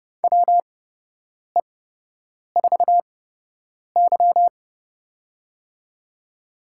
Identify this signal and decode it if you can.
Morse code, heard as a keyed tone, about 29 wpm, spaced as for 8 wpm, 700 Hz. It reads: WE4Y
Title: USA Call Signs - 4 Characters - 8x Character Spacing - 30wpm